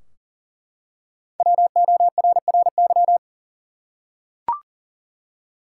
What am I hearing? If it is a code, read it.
Morse code, heard as a keyed tone, about 40 wpm, 700 Hz. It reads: WORRY
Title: Five Letter Words - 40wpm